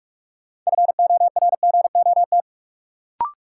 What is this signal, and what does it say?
Morse code, heard as a keyed tone, about 46 wpm, 700 Hz. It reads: FORGOT
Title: Sentences from Top 1400 Words - New Word Drill - 45wpm